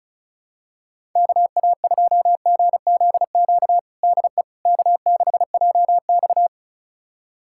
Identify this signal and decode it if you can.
Morse code, heard as a keyed tone, about 36 wpm, 700 Hz. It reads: KA2GZQ DE K6JX
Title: USA Callsign DE Callsign 35wpm